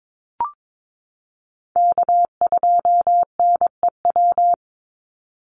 Morse code, heard as a keyed tone, about 22 wpm, 700 Hz. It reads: K2NEW
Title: USA Call Signs 22wpm